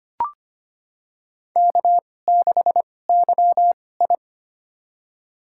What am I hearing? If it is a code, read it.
Morse code, heard as a keyed tone, about 25 wpm, spaced as for 18 wpm, 700 Hz. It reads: K6YI
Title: USA Call Signs - 2x Character Spacing - 25wpm